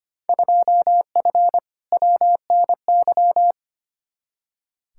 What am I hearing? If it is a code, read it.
Morse code, heard as a keyed tone, about 25 wpm, 700 Hz. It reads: 2F WNY
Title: Field Day Exchanges Encore - 25wpm